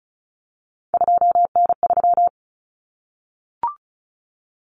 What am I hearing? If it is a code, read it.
Morse code, heard as a keyed tone, about 34 wpm, 700 Hz. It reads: 2N3
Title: Common Cut Numbers - 3-Digit 35wpm